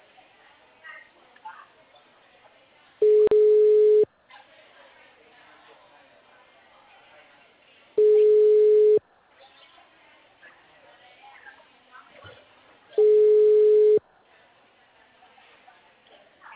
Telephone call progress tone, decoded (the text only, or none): ringback tone